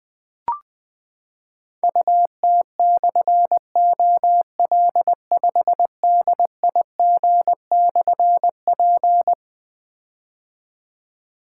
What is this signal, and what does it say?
Morse code, heard as a keyed tone, about 21 wpm, 700 Hz. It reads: UT/OL5DIG/P
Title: Difficult Call Signs 20wpm